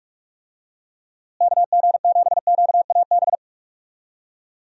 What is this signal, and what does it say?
Morse code, heard as a keyed tone, about 46 wpm, 700 Hz. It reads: KG7QAB
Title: USA Call Signs - 6 Characters - Repeated 3x - 45wpm